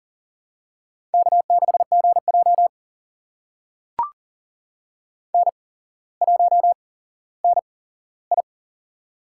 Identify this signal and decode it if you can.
Morse code, heard as a keyed tone, about 38 wpm, spaced as for 17 wpm, 700 Hz. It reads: K6GJ N1NI
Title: USA Call Signs - 4 Characters - 8x Character Spacing - 40wpm